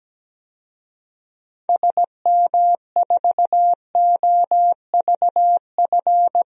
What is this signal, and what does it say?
Morse code, heard as a keyed tone, about 17 wpm, 700 Hz. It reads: SM4OVF
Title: DXCC-100 Call Signs - 6 Characters - 17wpm